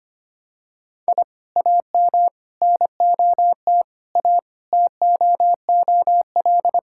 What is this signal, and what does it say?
Morse code, heard as a keyed tone, about 25 wpm, 700 Hz. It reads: I AM NOT A TOOL
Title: Sentences from Top 1000 Words - Review Part 6 - 25wpm